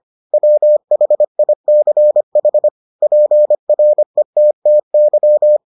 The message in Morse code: WHICH PRETTY

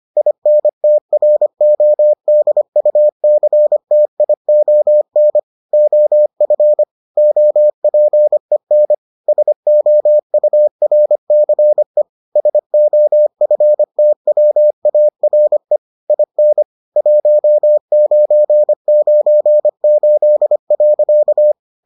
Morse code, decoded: INTRODUCTION OF OPEN SOURCE SOFTWARE IN 1998.

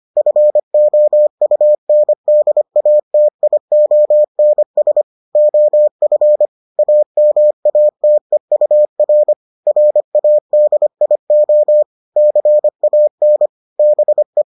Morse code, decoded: FOUNDATIONS OF AMATEUR RADIO CAN BE

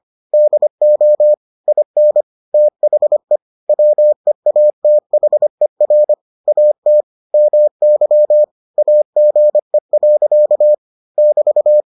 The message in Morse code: DO IN THE WEATHER AT MY AGE. <BT>

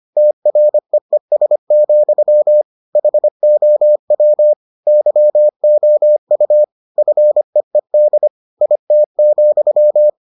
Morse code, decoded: TREES, HOW YOU FEED IT,

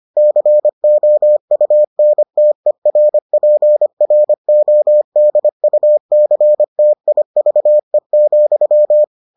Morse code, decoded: COUNTERPRODUCTIVE,

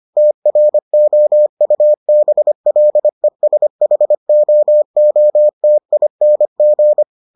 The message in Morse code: TROUBLESHOOTING

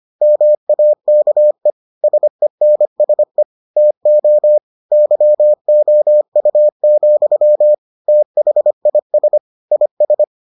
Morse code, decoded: MAKE SENSE TO YOU, THIS IS

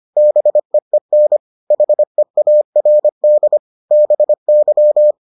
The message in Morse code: BEEN HEARD BY

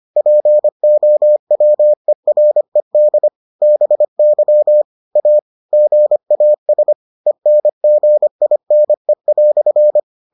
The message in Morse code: POWERED BY A GAS ENGINE"

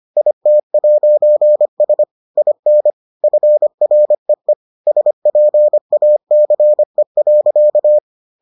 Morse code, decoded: IT'S IN FREE SPACE.

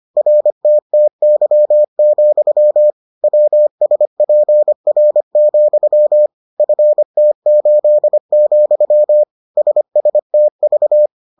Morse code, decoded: RTTY, WSPR, FT8, SSTV

